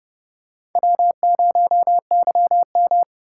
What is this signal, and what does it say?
Morse code, heard as a keyed tone, about 29 wpm, 700 Hz. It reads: W0YM
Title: USA Call Signs - 4 Characters - 8x Character Spacing - 30wpm